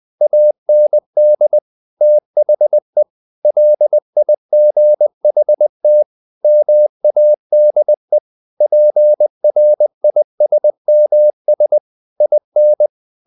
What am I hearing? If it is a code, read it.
Morse code, heard as a keyed tone, about 20 wpm, 600 Hz. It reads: AND THE LIGHT MADE PRISMS IN
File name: the old man and the sea / the old man and the sea part 1 20wpm